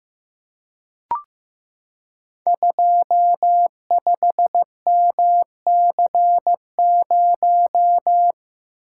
Morse code, heard as a keyed tone, about 15 wpm, 700 Hz. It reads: 25MC0
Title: Code Groups - 5 Letter:Numbers - 15wpm